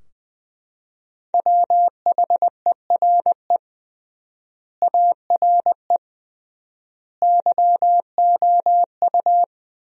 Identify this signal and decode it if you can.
Morse code, heard as a keyed tone, about 20 wpm, 700 Hz. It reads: WHERE ARE YOU
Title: 3x Word-Spacing - Sentences from Top 200 Words - 20wpm